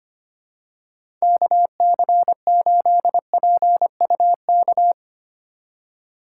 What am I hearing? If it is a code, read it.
Morse code, heard as a keyed tone, about 25 wpm, 700 Hz. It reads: KC8PUK
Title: USA Call Signs - 6 Characters - 8x Character Spacing - 25wpm